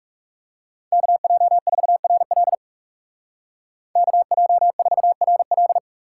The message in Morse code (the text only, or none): KJ4RL KJ4RL